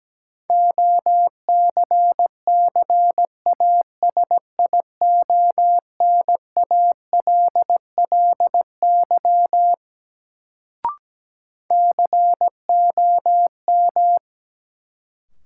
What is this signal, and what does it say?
Morse code, heard as a keyed tone, about 17 wpm, 700 Hz. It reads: OCCASIONALLY COM
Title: Sentences from Top 1300 Words - New Word Drill - 17wpm